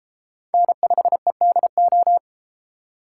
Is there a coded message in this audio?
Yes: Morse code from 0.5 to 2.2 s, heard as a keyed tone, about 32 wpm, 700 Hz. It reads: N5EDO